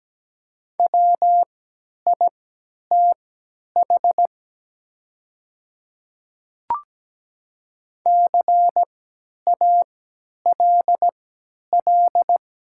Morse code, heard as a keyed tone, about 17 wpm, spaced as for 10 wpm, 700 Hz. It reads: WITH CALL
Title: 3x Character Spacing - Top 100 Words - 17wpm